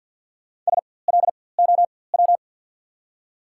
Morse code, heard as a keyed tone, about 54 wpm, spaced as for 24 wpm, 690 Hz. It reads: SLOW
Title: 4x Character Spacing - Top 500 Words - 50wpm